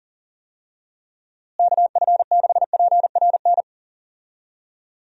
Morse code, heard as a keyed tone, about 40 wpm, 700 Hz. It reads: KF6PRN
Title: USA Call Signs - 6 Characters - Repeated 2x - 40wpm